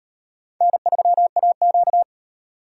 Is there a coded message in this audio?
Yes: Morse code from 0.6 to 2.0 s, heard as a keyed tone, about 38 wpm, 700 Hz. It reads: N3AQ